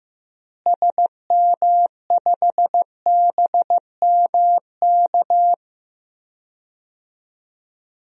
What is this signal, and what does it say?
Morse code, heard as a keyed tone, about 15 wpm, 700 Hz. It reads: SM5BMK
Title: DXCC-100 Call Signs - 6 Characters - Repeated 2x - 15wpm